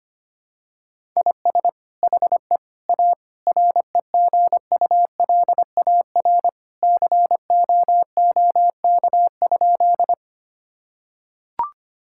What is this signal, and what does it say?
Morse code, heard as a keyed tone, about 25 wpm, 700 Hz. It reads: IS HE A REGULAR COOK?
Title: Sentences from Top 1000 Words - Review Part 6 - 25wpm